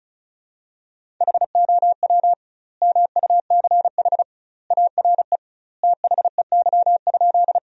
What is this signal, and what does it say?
Morse code, heard as a keyed tone, about 36 wpm, 700 Hz. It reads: HOW MUCH ARE THEY?
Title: 2x Word-Spacing - Sentences from Top 200 Words - 35wpm